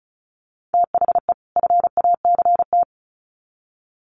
Morse code, heard as a keyed tone, about 34 wpm, 700 Hz. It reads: THE FACT
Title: Sets of 2 Words 35wpm